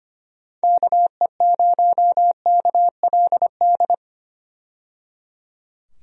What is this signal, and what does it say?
Morse code, heard as a keyed tone, about 25 wpm, 700 Hz. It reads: KE0KLD
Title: USA Call Signs - 6 Characters - 25wpm